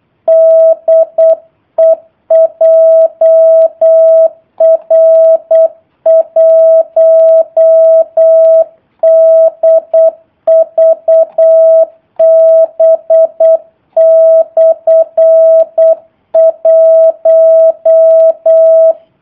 Morse code, heard as a keyed tone, about 8 wpm, 650 Hz. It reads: DE JR1DVB/1
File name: jh8cbh / 2019cw / sp03- w